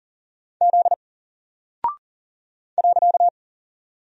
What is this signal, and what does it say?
Morse code, heard as a keyed tone, about 40 wpm, 700 Hz. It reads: Z .
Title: Single Character (Letter:Number:Punctuation:Prosign) Alt 40wpm